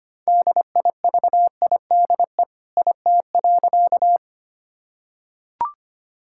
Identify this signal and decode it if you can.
Morse code, heard as a keyed tone, about 25 wpm, 700 Hz. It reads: DIVIDE IT.
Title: Sentences from Top 1600 Words 25wpm